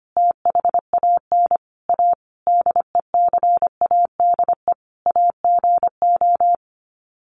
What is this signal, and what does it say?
Morse code, heard as a keyed tone, about 25 wpm, 700 Hz. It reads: THAN A DECADE AGO
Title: Sets of 5 Words Encore2 25wpm